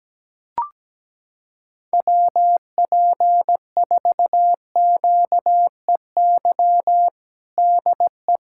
Morse code, heard as a keyed tone, about 17 wpm, 700 Hz. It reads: WP4QEY DE
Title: USA Callsign DE Callsign 17wpm